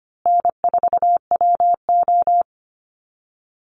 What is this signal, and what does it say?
Morse code, heard as a keyed tone, about 25 wpm, 700 Hz. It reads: N4WO
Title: Easy Call Signs Repeated 3x - 25wpm